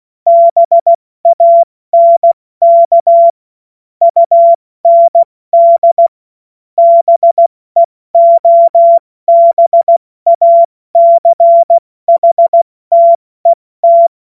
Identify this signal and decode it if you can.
Morse code, heard as a keyed tone, about 16 wpm, 680 Hz. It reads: BANK UND BEOBACHTET
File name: Morsekurs / Witze / Witz 22. August 2023 70 F80